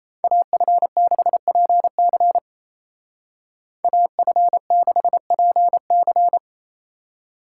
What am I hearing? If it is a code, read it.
Morse code, heard as a keyed tone, about 36 wpm, 700 Hz. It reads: AF6PC AF6PC